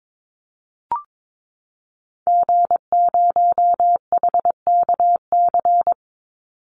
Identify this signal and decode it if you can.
Morse code, heard as a keyed tone, about 22 wpm, 700 Hz. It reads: G0HKC
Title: Call Signs Repeated 2x - 22wpm